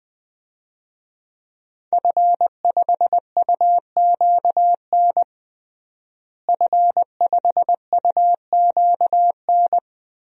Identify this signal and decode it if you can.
Morse code, heard as a keyed tone, about 20 wpm, 700 Hz. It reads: F5UQN F5UQN
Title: DXCC-100 Call Signs - 5 Characters - Repeated 3x - 20wpm